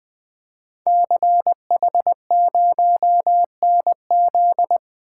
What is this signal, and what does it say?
Morse code, heard as a keyed tone, about 20 wpm, 700 Hz. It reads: CH0NZ
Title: Code Groups - 5 Letter:Numbers - 20wpm